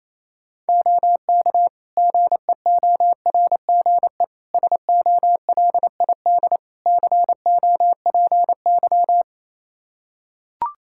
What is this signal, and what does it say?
Morse code, heard as a keyed tone, about 28 wpm, 700 Hz. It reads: OK GEORGE SOLID COPY
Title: K1USN SST QSO Practice 28wpm